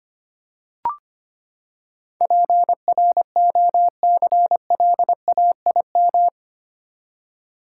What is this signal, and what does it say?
Morse code, heard as a keyed tone, about 25 wpm, 700 Hz. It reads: PROCLAIM
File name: Eight Letter Words - 25wpm